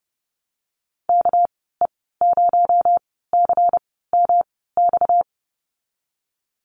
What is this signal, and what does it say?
Morse code, heard as a keyed tone, about 30 wpm, spaced as for 17 wpm, 700 Hz. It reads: KE0CMX
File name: Lesson 37g - Letter x - Call Signs 30wpm